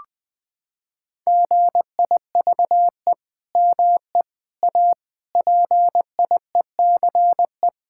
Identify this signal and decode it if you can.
Morse code, heard as a keyed tone, about 20 wpm, 700 Hz. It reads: GIVE ME A PIECE
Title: Sentences from Top 1600 Words 20wpm